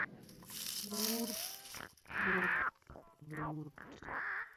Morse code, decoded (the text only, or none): none